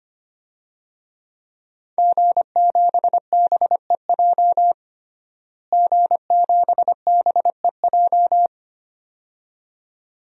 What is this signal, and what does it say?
Morse code, heard as a keyed tone, about 25 wpm, 700 Hz. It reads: G7BEJ G7BEJ
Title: DXCC-100 Call Signs - 5 Characters - Repeated 2x - 25wpm